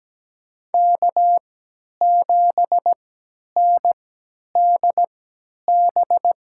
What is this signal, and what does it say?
Morse code, heard as a keyed tone, about 17 wpm, spaced as for 10 wpm, 700 Hz. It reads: K7NDB
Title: USA Call Signs - 5 Characters - 3x Character Spacing - 17wpm